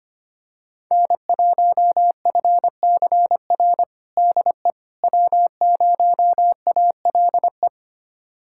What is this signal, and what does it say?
Morse code, heard as a keyed tone, about 25 wpm, 700 Hz. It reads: N1FCR DE W0ALE
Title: USA Callsign DE Callsign - 5 Characters - 25wpm